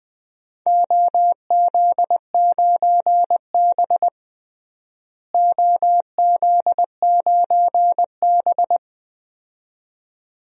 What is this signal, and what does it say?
Morse code, heard as a keyed tone, about 20 wpm, 700 Hz. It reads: OZ9B OZ9B
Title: DXCC-100 Call Signs - 4 Characters - Repeated 2x - 20wpm